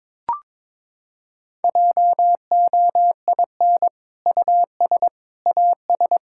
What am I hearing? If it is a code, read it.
Morse code, heard as a keyed tone, about 22 wpm, 700 Hz. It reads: JOIN US AS
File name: practice / 22wpm / Sentences with 2-Word Phrases - 22wpm